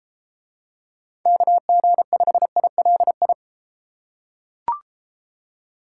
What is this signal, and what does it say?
Morse code, heard as a keyed tone, about 33 wpm, 700 Hz. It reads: KG5ILI